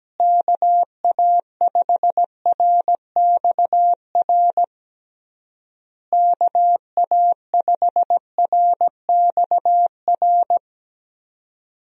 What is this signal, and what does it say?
Morse code, heard as a keyed tone, about 17 wpm, 700 Hz. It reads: KA5RXR KA5RXR
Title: USA Call Signs - 6 Characters - Repeated 3x - 17wpm